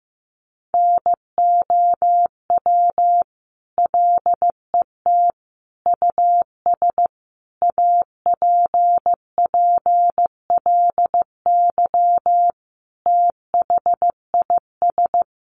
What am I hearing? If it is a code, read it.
Morse code, heard as a keyed tone, about 14 wpm, 700 Hz. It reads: NOW LET US APPLY THIS
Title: Sentences from Top 1200 Words 15wpm